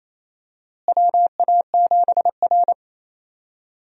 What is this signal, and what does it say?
Morse code, heard as a keyed tone, about 28 wpm, 700 Hz. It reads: WA7R
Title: USA Call Signs - 4 Characters - Repeated 2x - 28wpm